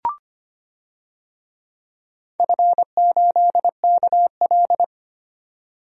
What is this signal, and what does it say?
Morse code, heard as a keyed tone, about 25 wpm, 700 Hz. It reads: F8KL
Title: DXCC-100 Call Signs - 4 Characters - 25wpm